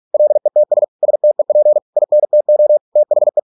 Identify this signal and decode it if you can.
Morse code, heard as a keyed tone, about 46 wpm, 600 Hz. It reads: LETS STEP INTO THE